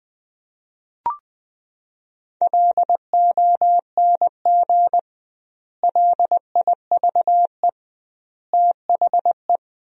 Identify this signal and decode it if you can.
Morse code, heard as a keyed tone, about 20 wpm, 700 Hz. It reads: LONG LIVE THE
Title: 2x Word-Spacing - Sentences from Top 400 Words - 20wpm